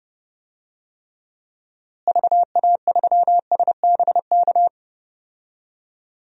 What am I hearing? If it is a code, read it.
Morse code, heard as a keyed tone, about 32 wpm, 700 Hz. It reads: VA3SBK
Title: DXCC-100 Call Signs - 6 Characters - Repeated 2x - 30wpm